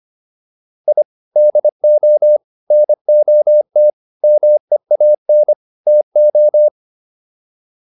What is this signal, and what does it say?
Morse code, heard as a keyed tone, about 25 wpm, 600 Hz. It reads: I DO NOT MEAN TO